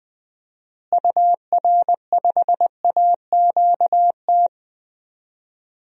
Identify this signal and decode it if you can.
Morse code, heard as a keyed tone, about 20 wpm, 700 Hz. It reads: UR5AQT
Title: DXCC-100 Call Signs - 6 Characters - Repeated 2x - 20wpm